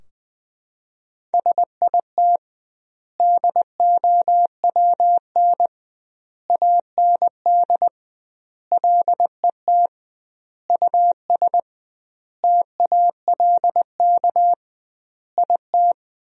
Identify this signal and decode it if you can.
Morse code, heard as a keyed tone, about 20 wpm, 700 Hz. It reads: SIT DOWN AND LET US TALK IT
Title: 2x Word-Spacing - Sentences from Top 500 Words - 20wpm